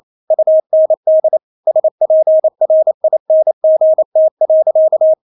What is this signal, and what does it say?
Morse code, heard as a keyed tone, about 28 wpm, 630 Hz. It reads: UND SPRINGT.
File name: Witz9 28wpm